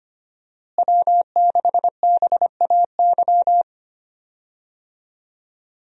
Morse code, heard as a keyed tone, about 25 wpm, 700 Hz. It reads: W6BAY
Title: USA Call Signs - 5 Characters - Repeated 2x - 25wpm